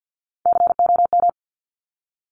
Morse code, heard as a keyed tone, about 50 wpm, 700 Hz. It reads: COM
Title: Top Three Letter Combinations in Words 50wpm